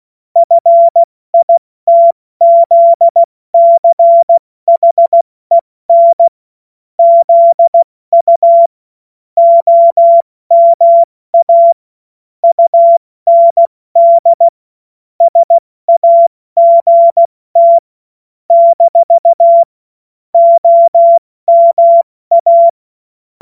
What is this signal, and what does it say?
Morse code, heard as a keyed tone, about 16 wpm, 680 Hz. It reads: FITZCHEN ZU OMA UND SAGT - OMA
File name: Morsekurs / Witze / Witz 28. September 2021, 70 F80